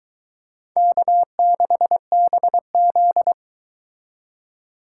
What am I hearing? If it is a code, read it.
Morse code, heard as a keyed tone, about 22 wpm, 700 Hz. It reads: K6BZ